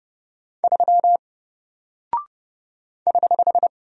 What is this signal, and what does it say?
Morse code, heard as a keyed tone, about 30 wpm, 700 Hz. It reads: 3 <HH>
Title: Single Character (Letter:Number:Punctuation:Prosign) Alt 30wpm